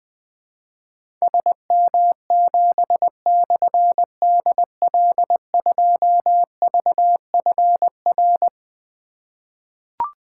Morse code, heard as a keyed tone, about 21 wpm, 700 Hz. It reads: SM7/DL2VFR
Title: Difficult Call Signs 20wpm